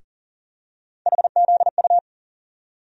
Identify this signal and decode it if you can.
Morse code, heard as a keyed tone, about 38 wpm, 700 Hz. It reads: HZU
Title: USA Call Sign Postfixes - 3 Letters 40wpm